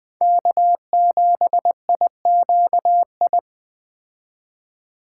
Morse code, heard as a keyed tone, about 21 wpm, 700 Hz. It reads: K7IQI
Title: Speed Racing - USA Call Signs 17wpm